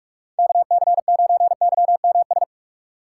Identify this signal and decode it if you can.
Morse code, heard as a keyed tone, about 47 wpm, 700 Hz. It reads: KC9YMS
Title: USA Call Signs - 6x Character Spacing - 45wpm